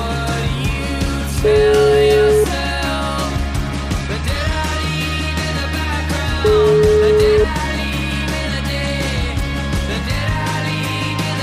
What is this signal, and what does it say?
Ringback tone (Europe, 425 Hz), with noise.